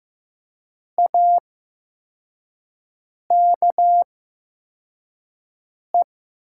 Morse code, heard as a keyed tone, about 15 wpm, spaced as for 4 wpm, 700 Hz. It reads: AKE
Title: 8x Character Spacing - Top 100 Words - 15wpm